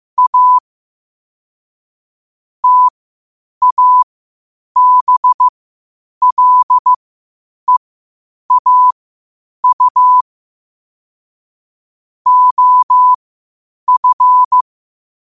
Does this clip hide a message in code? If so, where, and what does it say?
Yes: Morse code from 0.2 to 14.6 s, heard as a keyed tone, about 15 wpm, spaced as for 8 wpm, 1000 Hz. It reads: A TABLEAU OF